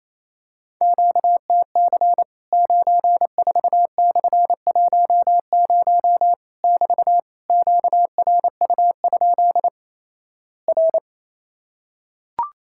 Morse code, heard as a keyed tone, about 28 wpm, 700 Hz. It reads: QTC 94/10 <BT> QRU? R
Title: Worked All Europe DX Contest - QTC Exchange - 28wpm